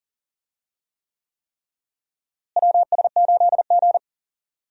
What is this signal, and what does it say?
Morse code, heard as a keyed tone, about 38 wpm, 700 Hz. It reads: WS8G
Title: USA Call Signs - 4 Characters - Repeated 3x - 40wpm